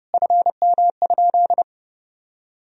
Morse code, heard as a keyed tone, about 30 wpm, 700 Hz. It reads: FM?
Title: Sentences from Top 400 Words and Abbreviations 30wpm